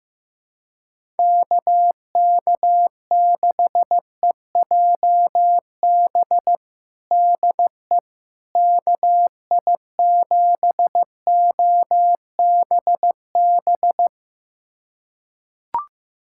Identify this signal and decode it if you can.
Morse code, heard as a keyed tone, about 16 wpm, 700 Hz. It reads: KK6EJB DE KI7OBB
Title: USA Callsign DE Callsign - 6 Characters - 15wpm